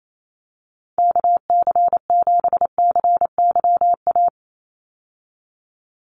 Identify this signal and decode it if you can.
Morse code, heard as a keyed tone, about 28 wpm, 700 Hz. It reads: KC7CYA
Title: USA Call Signs 28wpm